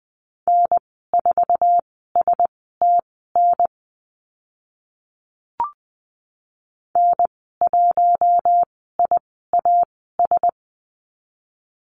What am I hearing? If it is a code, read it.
Morse code, heard as a keyed tone, about 20 wpm, spaced as for 14 wpm, 700 Hz. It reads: N4STN N1IAS
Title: 062 Lesson 11g - Letter h - Call Signs 20wpm